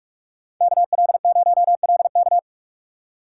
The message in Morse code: KL0LK

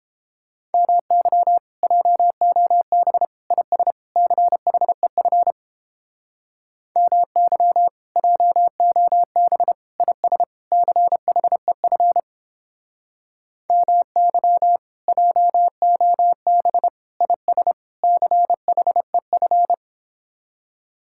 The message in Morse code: MY JOB IS CHEF MY JOB IS CHEF MY JOB IS CHEF

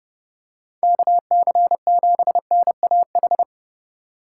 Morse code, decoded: KC7NAH

